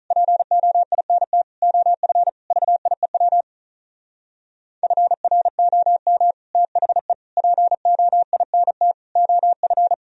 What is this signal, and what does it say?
Morse code, heard as a keyed tone, about 42 wpm, 700 Hz. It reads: POINT OF VIEW FROM THE POINT OF
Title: Speed-Racing - Sets of 5 Words 35wpm